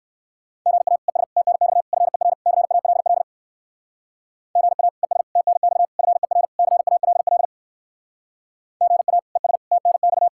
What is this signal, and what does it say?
Morse code, heard as a keyed tone, about 68 wpm, 700 Hz. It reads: GA ES TNX FER CALL GA ES TNX FER CALL GA ES TNX